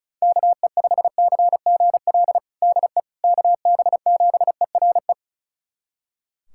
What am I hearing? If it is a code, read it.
Morse code, heard as a keyed tone, about 36 wpm, 700 Hz. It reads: KE5CGL DE KB7ERE
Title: USA Callsign DE Callsign - 6 Characters - 35wpm